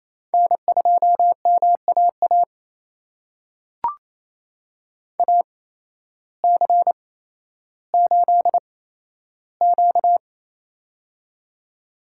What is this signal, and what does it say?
Morse code, heard as a keyed tone, about 28 wpm, spaced as for 13 wpm, 700 Hz. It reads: N2MAA AC8Q
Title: USA Call Signs - 8x Character Spacing - 28wpm